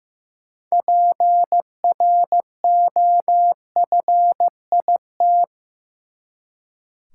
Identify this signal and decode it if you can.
Morse code, heard as a keyed tone, about 15 wpm, 700 Hz. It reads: PROFIT
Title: Six Letter Words - 15wpm